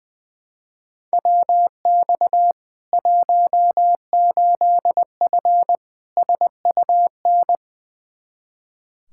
Morse code, heard as a keyed tone, about 20 wpm, 700 Hz. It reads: WX 18F SUN
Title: Practice QSO - Exchange 2 - Info - 20wpm